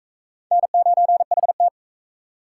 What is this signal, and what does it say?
Morse code, heard as a keyed tone, about 37 wpm, 700 Hz. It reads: N9HT